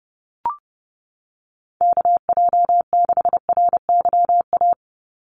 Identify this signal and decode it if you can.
Morse code, heard as a keyed tone, about 30 wpm, 700 Hz. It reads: KJ6RYA
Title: USA Call Signs 30wpm